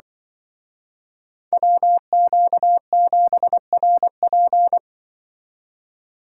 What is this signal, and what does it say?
Morse code, heard as a keyed tone, about 24 wpm, 700 Hz. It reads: WQ7RP
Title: Speed Racing - USA 2x2 Call Signs 20wpm